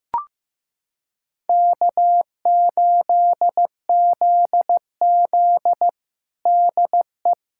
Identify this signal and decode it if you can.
Morse code, heard as a keyed tone, about 15 wpm, 700 Hz. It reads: K8ZZ DE
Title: Practice QSO - Exchange 4 - Ending - 15wpm